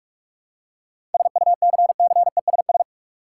Morse code, heard as a keyed tone, about 47 wpm, 700 Hz. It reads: SUCCESS
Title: Seven Letter Words - 45wpm